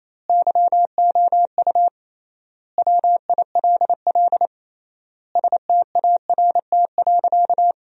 Morse code, heard as a keyed tone, about 28 wpm, 700 Hz. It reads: YOU WILL START.
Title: 3x Word-Spacing - Sentences from Top 300 Words - 28wpm